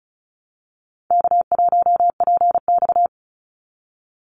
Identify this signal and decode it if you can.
Morse code, heard as a keyed tone, about 34 wpm, 700 Hz. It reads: K1PX
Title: Easy Call Signs Repeated 2x - 35wpm